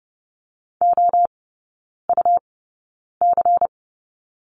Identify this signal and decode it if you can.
Morse code, heard as a keyed tone, about 30 wpm, spaced as for 9 wpm, 700 Hz. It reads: OUC
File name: Lesson 19e - Letter w - Three Characters 30wpm